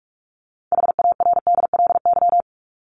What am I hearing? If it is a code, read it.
Morse code, heard as a keyed tone, about 44 wpm, 700 Hz. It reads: HARDLY
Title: Top 1751-2000 Words 45wpm